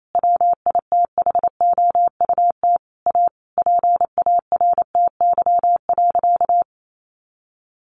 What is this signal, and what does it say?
Morse code, heard as a keyed tone, about 28 wpm, 700 Hz. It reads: WITHOUT A PARTY.
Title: Sentences from Top 1500 Words 28wpm